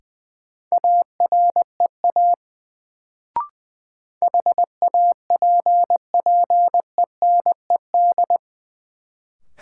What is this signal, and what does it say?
Morse code, heard as a keyed tone, about 20 wpm, 700 Hz. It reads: AREA HAPPENED
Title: Sentences from Top 1000 Words - Review Part 2 - New Word Drill - 20wpm